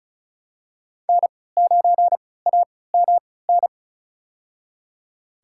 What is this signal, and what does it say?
Morse code, heard as a keyed tone, about 34 wpm, spaced as for 20 wpm, 700 Hz. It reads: N9AMN
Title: USA Call Signs - 5 Characters - 3x Character Spacing - 35wpm